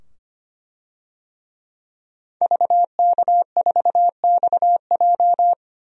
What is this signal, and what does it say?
Morse code, heard as a keyed tone, about 25 wpm, 700 Hz. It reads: VK4XJ
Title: DXCC-100 Call Signs - 5 Characters - 25wpm